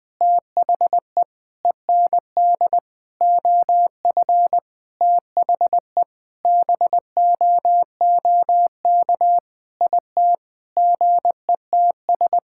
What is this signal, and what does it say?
Morse code, heard as a keyed tone, about 20 wpm, 700 Hz. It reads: THE END OF THE BOOK IT GETS
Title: Sentences with 5-Word Phrases - 20wpm